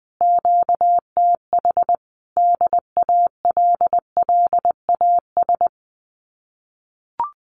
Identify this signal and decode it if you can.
Morse code, heard as a keyed tone, about 20 wpm, 700 Hz. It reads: QTH DALLAS
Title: Common QSO Phrases Encore 20wpm